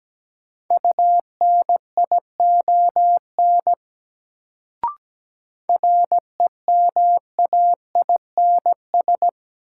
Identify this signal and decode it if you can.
Morse code, heard as a keyed tone, about 17 wpm, 700 Hz. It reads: UNION REMAINS
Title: Sets of 5 Words Encore - New Word Drill - 17wpm